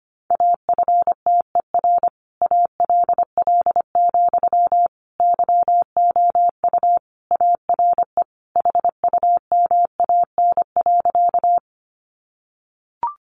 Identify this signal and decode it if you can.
Morse code, heard as a keyed tone, about 25 wpm, 700 Hz. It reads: AFTER ALL, YOU ARE HUMAN.
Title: Sentences from Top 600 Words 25wpm